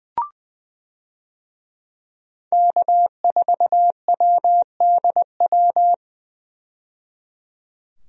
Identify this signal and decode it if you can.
Morse code, heard as a keyed tone, about 21 wpm, 700 Hz. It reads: K4WDW
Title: USA Call Signs - 5 Characters - 20wpm